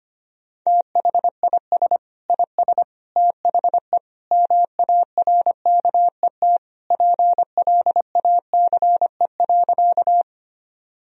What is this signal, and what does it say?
Morse code, heard as a keyed tone, about 25 wpm, 700 Hz. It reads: THIS IS THE MARKET PLACE.
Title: Sentences from Top 1000 Words - Review Part 3 - 25wpm